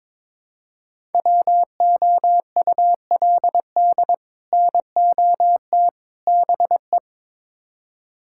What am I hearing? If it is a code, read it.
Morse code, heard as a keyed tone, about 21 wpm, 700 Hz. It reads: WOULD NOT BE
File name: practice / 22wpm / Build A Phrase - 5-Word Phrases - 22wpm